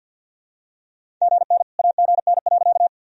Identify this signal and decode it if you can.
Morse code, heard as a keyed tone, about 50 wpm, 700 Hz. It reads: GN AGN.
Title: Sentences from Top 300 Words and Abbreviations 50wpm